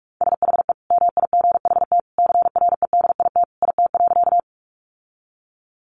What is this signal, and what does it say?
Morse code, heard as a keyed tone, about 44 wpm, 700 Hz. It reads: SHE MIGHT CREDIT IT.
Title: Sentences from Top 1000 Words 45wpm